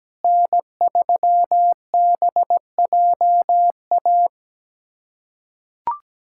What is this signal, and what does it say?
Morse code, heard as a keyed tone, about 17 wpm, 700 Hz. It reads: N3BJA
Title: USA Callsign DE Callsign - 5 Characters - 17wpm